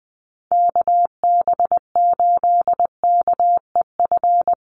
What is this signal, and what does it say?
Morse code, heard as a keyed tone, about 20 wpm, 700 Hz. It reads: KB8KEF
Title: USA Call Signs Repeated 3x 20wpm